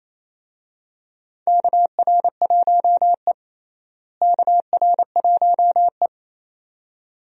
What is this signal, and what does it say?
Morse code, heard as a keyed tone, about 28 wpm, 700 Hz. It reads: KR1E KR1E
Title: USA Call Signs - 4 Characters - Repeated 2x - 28wpm